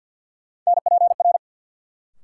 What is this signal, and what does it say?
Morse code, heard as a keyed tone, about 53 wpm, 700 Hz. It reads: NPR
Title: Sets of 5 Words - New Word Drill - 50wpm